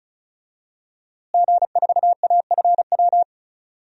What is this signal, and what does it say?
Morse code, heard as a keyed tone, about 36 wpm, 700 Hz. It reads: G4AFW
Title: DXCC-100 Call Signs - 5 Characters - Repeated 3x - 35wpm